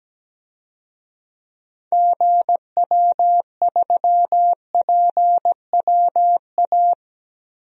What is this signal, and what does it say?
Morse code, heard as a keyed tone, about 17 wpm, 700 Hz. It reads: GW3PWA
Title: DXCC-100 Call Signs - 6 Characters - Repeated 3x - 17wpm